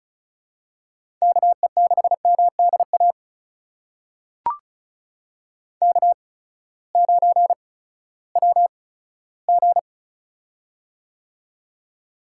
Morse code, heard as a keyed tone, about 36 wpm, spaced as for 18 wpm, 700 Hz. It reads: KE6MDA K9WG
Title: USA Call Signs - 8x Character Spacing - 35wpm